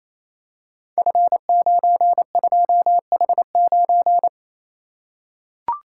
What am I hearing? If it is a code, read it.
Morse code, heard as a keyed tone, about 29 wpm, 700 Hz. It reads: F92H9
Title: Code Groups - 5 Letter:Numbers - 28wpm